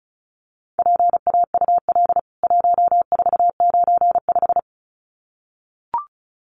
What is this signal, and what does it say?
Morse code, heard as a keyed tone, about 34 wpm, 700 Hz. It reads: PAUL 1495
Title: CWT Exchange 35wpm